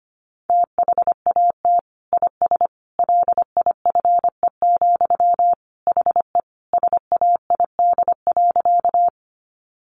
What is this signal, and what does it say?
Morse code, heard as a keyed tone, about 25 wpm, 700 Hz. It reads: THAT IS LIFE, HE SAID.
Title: Sentences from Top 900 Words 25wpm